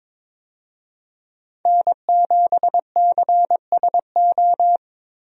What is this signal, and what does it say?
Morse code, heard as a keyed tone, about 22 wpm, 700 Hz. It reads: N7CSO
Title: USA Call Signs - 5 Characters - Repeated 3x - 22wpm